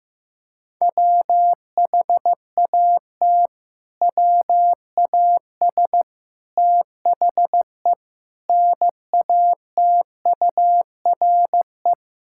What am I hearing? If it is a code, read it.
Morse code, heard as a keyed tone, about 15 wpm, 700 Hz. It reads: WHAT WAS THE NATURE
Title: Sentences from Top 1000 Words - Review Part 6 - 15wpm